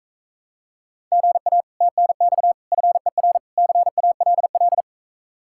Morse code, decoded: GA TNX FER CALL

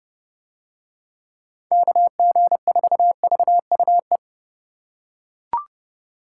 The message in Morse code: KG4VUE

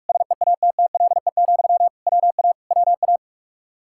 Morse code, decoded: SEATTLE, WA WA

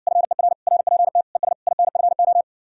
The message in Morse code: FER RPT ES INFO

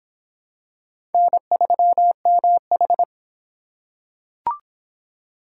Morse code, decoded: N3MH

